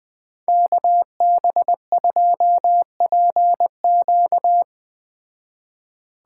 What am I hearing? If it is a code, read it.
Morse code, heard as a keyed tone, about 20 wpm, 700 Hz. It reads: KB2PQ